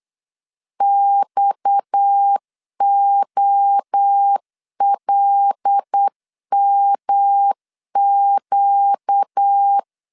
Morse code, heard as a keyed tone, about 8 wpm, 800 Hz. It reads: XOLMQ
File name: LEZIONI / 6 PKX0 / 6 PKX 8wpm 3041